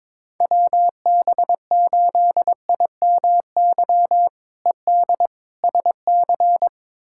Morse code, heard as a keyed tone, about 22 wpm, 700 Hz. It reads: WB8IMY ED SC
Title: K1USN SST QSO Practice - 22wpm